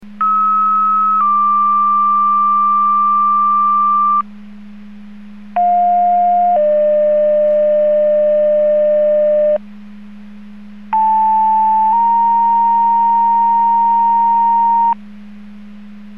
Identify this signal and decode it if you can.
Two-tone page, heard as a one-second tone, then a three-second tone, about 1290 then 1180 Hz, about 710 then 600 Hz, about 880 then 930 Hz. It is heard from 0.2 to 14.9 s.